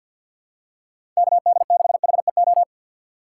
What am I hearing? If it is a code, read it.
Morse code, heard as a keyed tone, about 50 wpm, 700 Hz. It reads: KD6HEO
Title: USA Call Signs - 6 Characters - 4x Character Spacing - 50wpm